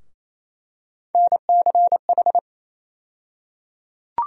Morse code, heard as a keyed tone, about 28 wpm, 700 Hz. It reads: NCH
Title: USA Call Sign Postfixes - 3 Letters 28wpm